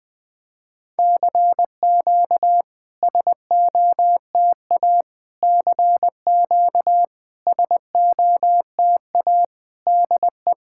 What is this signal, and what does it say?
Morse code, heard as a keyed tone, about 21 wpm, 700 Hz. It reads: CQ SOTA CQ SOTA DE
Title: SOTA QSO Practice 20wpm